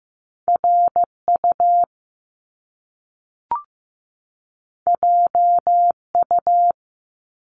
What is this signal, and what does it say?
Morse code, heard as a keyed tone, about 15 wpm, 700 Hz. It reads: RU JU
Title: Top Two Letter Combinations in Words 15wpm